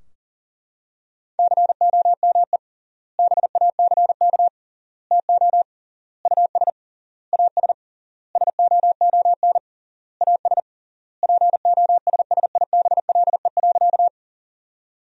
Morse code, heard as a keyed tone, about 40 wpm, 700 Hz. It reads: COME BACK TO US AS SOON AS POSSIBLE.
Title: 3x Word-Spacing - Sentences from Top 500 Words - 40wpm